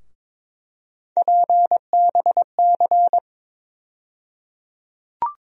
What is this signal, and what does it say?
Morse code, heard as a keyed tone, about 22 wpm, 700 Hz. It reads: PBC
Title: USA Call Sign Postfixes - 3 Letters 22wpm